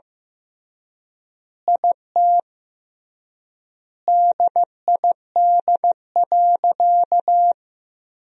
Morse code, heard as a keyed tone, about 15 wpm, 700 Hz. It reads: IT DID.
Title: 3x Word-Spacing - Sentences from Top 500 Words - 15wpm